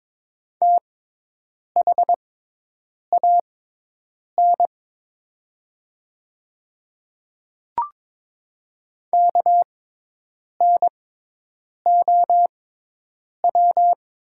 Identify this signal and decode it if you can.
Morse code, heard as a keyed tone, about 22 wpm, spaced as for 8 wpm, 700 Hz. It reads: THAN KNOW
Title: 6x Character Spacing - Top 500 Words - 22wpm